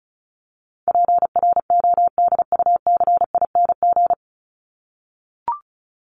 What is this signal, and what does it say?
Morse code, heard as a keyed tone, about 34 wpm, 700 Hz. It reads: PRODUCING
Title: ING Suffix - Top 100 Words with Base Words 35wpm